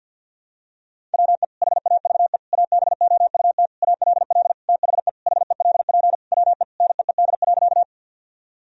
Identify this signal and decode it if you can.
Morse code, heard as a keyed tone, about 50 wpm, 700 Hz. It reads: WE HAVE ABOUT ALL THE HELP WE NEED.
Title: Sentences from Top 1000 Words - Review Part 1 - 50wpm